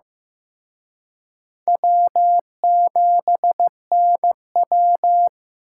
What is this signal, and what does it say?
Morse code, heard as a keyed tone, about 18 wpm, 700 Hz. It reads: W7NW
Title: Speed Racing - USA 1x2 Call Signs 15wpm